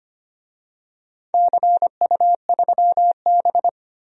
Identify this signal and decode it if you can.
Morse code, heard as a keyed tone, about 25 wpm, 700 Hz. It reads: CU3B
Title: DXCC-100 Call Signs - 4 Characters - 25wpm